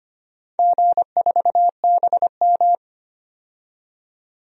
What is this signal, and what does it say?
Morse code, heard as a keyed tone, about 25 wpm, 700 Hz. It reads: G4BM
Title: DXCC-100 Call Signs - 4 Characters - 25wpm